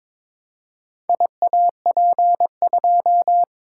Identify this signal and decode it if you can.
Morse code, heard as a keyed tone, about 22 wpm, 700 Hz. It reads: IAP2
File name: Four Letter-Number Combinations - 22wpm